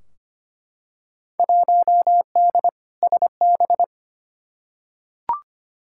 Morse code, heard as a keyed tone, about 25 wpm, 700 Hz. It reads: 1D SB
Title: Field Day Exchanges - 25wpm